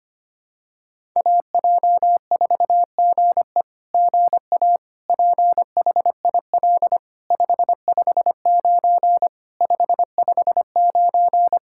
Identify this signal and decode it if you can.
Morse code, heard as a keyed tone, about 25 wpm, 700 Hz. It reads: AJ4GE GA PHIL 559 559
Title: SOTA QSO Practice 25wpm